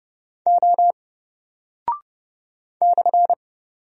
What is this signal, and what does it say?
Morse code, heard as a keyed tone, about 30 wpm, 700 Hz. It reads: O /
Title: Single Character (Letter:Number:Punctuation:Prosign) 30wpm